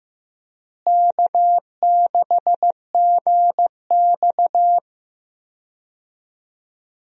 Morse code, heard as a keyed tone, about 15 wpm, 700 Hz. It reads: K6GX
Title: USA Call Signs - 4 Characters - Repeated 2x - 15wpm